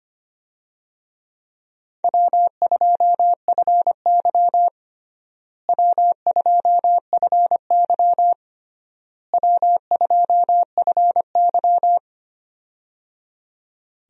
Morse code, heard as a keyed tone, about 25 wpm, 700 Hz. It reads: W2FY W2FY W2FY
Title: USA Call Signs - 4 Characters - Repeated 3x - 25wpm